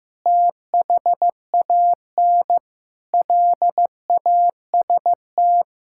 Morse code, heard as a keyed tone, about 16 wpm, 700 Hz. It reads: THAN LAST
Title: Sentences from Top 1000 Words - Review Part 5 - 15wpm